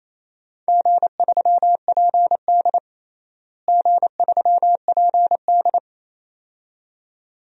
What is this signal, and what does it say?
Morse code, heard as a keyed tone, about 28 wpm, 700 Hz. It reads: G3PD G3PD
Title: DXCC-100 Call Signs - 4 Characters - Repeated 2x - 28wpm